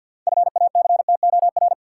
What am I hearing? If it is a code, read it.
Morse code, heard as a keyed tone, about 53 wpm, 700 Hz. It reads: FACTOR
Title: Sentences from Top 1300 Words - New Word Drill Alt - 50wpm